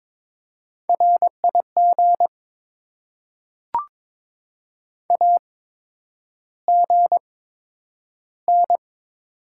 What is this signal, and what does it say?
Morse code, heard as a keyed tone, about 22 wpm, spaced as for 9 wpm, 700 Hz. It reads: RIG AGN
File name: Top 20 QSO Elements - 8x Character Spacing - 22wpm